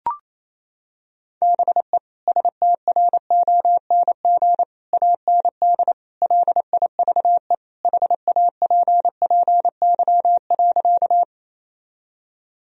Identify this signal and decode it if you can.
Morse code, heard as a keyed tone, about 28 wpm, 700 Hz. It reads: BE STRONG AND LIVE HAPPY.
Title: Sentences from Top 1000 Words - Review Part 3 - 28wpm